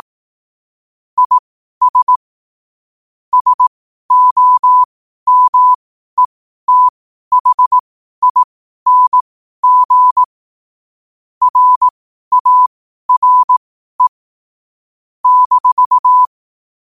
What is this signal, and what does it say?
Morse code, heard as a keyed tone, about 18 wpm, spaced as for 12 wpm, 1000 Hz. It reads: IS SOMETHING RARE -
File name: qotdfiles / 12wpm